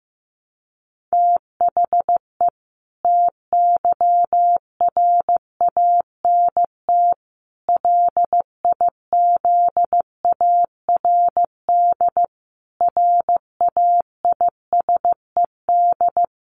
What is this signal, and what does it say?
Morse code, heard as a keyed tone, about 15 wpm, 700 Hz. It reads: THE TYRANT LIZARD RAISED
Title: A Sound Of Thunder 15wpm